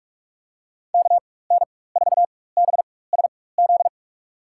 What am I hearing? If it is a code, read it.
Morse code, heard as a keyed tone, about 41 wpm, spaced as for 20 wpm, 700 Hz. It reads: KN4BSZ
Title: USA Call Signs - 6 Characters - 4x Character Spacing - 45wpm